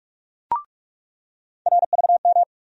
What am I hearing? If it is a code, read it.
Morse code, heard as a keyed tone, about 46 wpm, 700 Hz. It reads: RVM
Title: USA Call Sign Postfixes - 3 Letters 45wpm